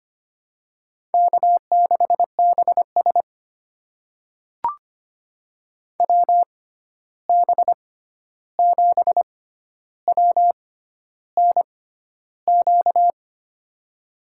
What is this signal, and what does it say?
Morse code, heard as a keyed tone, about 25 wpm, spaced as for 11 wpm, 700 Hz. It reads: K6BS WB7WNQ
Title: USA Call Signs - 6x Character Spacing - 25wpm